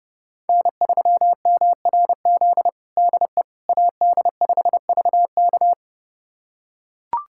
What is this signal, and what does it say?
Morse code, heard as a keyed tone, about 30 wpm, 700 Hz. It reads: N3MRZ DE AD5VK
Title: USA Callsign DE Callsign - 5 Characters - 30wpm